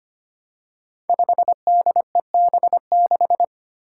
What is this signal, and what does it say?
Morse code, heard as a keyed tone, about 25 wpm, 700 Hz. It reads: 5DEB6
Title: Five Challenging Letter-Number Combinations 25wpm